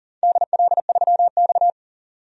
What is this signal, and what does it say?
Morse code, heard as a keyed tone, about 42 wpm, 700 Hz. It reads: DL3X
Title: DXCC-100 Call Signs - 4 Characters - 40wpm